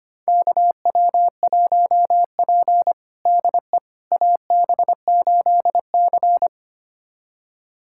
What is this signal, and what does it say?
Morse code, heard as a keyed tone, about 25 wpm, 700 Hz. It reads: KW1P DE AB8C
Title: USA Callsign DE Callsign - 4 Characters - 25wpm